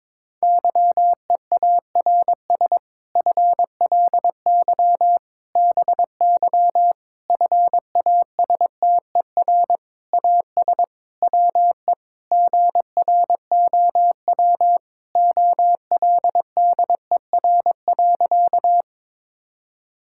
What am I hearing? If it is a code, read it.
Morse code, heard as a keyed tone, about 22 wpm, 700 Hz. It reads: YEARS FLY BY FASTER AS WE GROW OLDER.
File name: Difficult Sentences with Hints - 22wpm